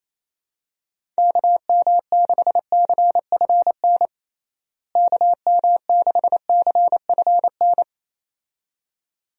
Morse code, heard as a keyed tone, about 28 wpm, 700 Hz. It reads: KM6CFN KM6CFN
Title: USA Call Signs - 6 Characters - Repeated 2x - 28wpm